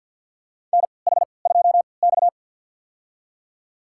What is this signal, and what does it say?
Morse code, heard as a keyed tone, about 47 wpm, spaced as for 28 wpm, 700 Hz. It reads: NH2X
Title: USA Call Signs - 4 Characters - 3x Character Spacing - 50wpm